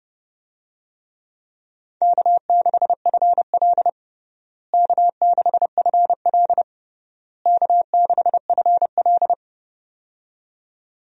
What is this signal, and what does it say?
Morse code, heard as a keyed tone, about 30 wpm, 700 Hz. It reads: K6FL K6FL K6FL
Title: USA Call Signs - 4 Characters - Repeated 3x - 30wpm